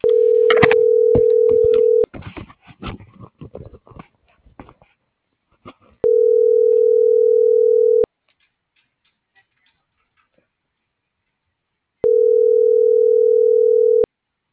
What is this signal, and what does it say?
Ringback tone (North America), 440 + 480 Hz.